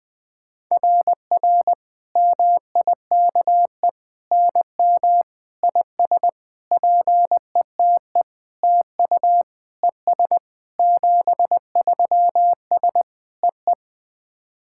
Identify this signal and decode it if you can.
Morse code, heard as a keyed tone, about 20 wpm, 690 Hz. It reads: RR MIKE NM IS PETE TU ES 73S EE
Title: SOTA QSO Practice 20wpm